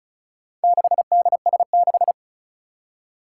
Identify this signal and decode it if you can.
Morse code, heard as a keyed tone, about 34 wpm, 700 Hz. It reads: 6DS6
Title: Four Challenging Letter-Number Combinations 35wpm